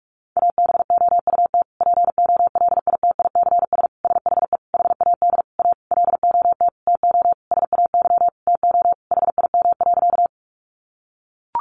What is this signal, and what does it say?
Morse code, heard as a keyed tone, about 44 wpm, 700 Hz. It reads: ABOUT POLITICS SHE HAD A LOT TO SAY TO HIM.
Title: Sentences from Top 1000 Words 45wpm